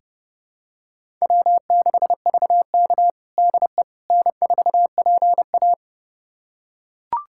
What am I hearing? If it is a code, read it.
Morse code, heard as a keyed tone, about 30 wpm, 700 Hz. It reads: W6VK DE N4PA
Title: USA Callsign DE Callsign - 4 Characters - 30wpm